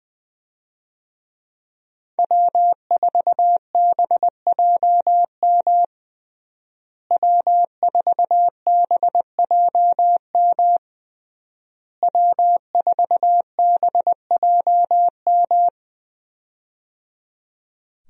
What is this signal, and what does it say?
Morse code, heard as a keyed tone, about 20 wpm, 700 Hz. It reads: W4BJM W4BJM W4BJM
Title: USA Call Signs - 5 Characters - Repeated 3x - 20wpm